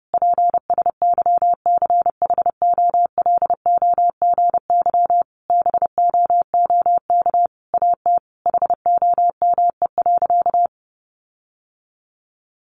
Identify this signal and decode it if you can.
Morse code, heard as a keyed tone, about 28 wpm, 700 Hz. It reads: PSYCHOLOGY BOOK AT HOME.
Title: Sentences from Top 1400 Words 30wpm